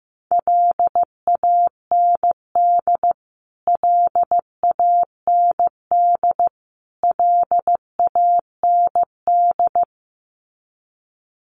Morse code, heard as a keyed tone, about 15 wpm, 700 Hz. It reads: LAND LAND LAND
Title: Top 200 Words Repeated 15wpm